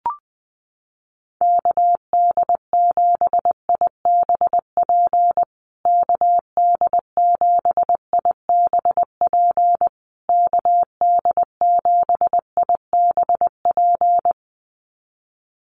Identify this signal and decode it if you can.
Morse code, heard as a keyed tone, about 20 wpm, 700 Hz. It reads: KD7IBP KD7IBP KD7IBP
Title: USA Call Signs Repeated 3x 20wpm